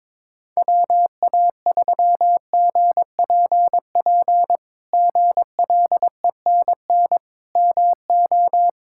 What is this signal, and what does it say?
Morse code, heard as a keyed tone, about 22 wpm, 700 Hz. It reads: WA3GPP GLENN MO
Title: K1USN SST QSO Practice - 22wpm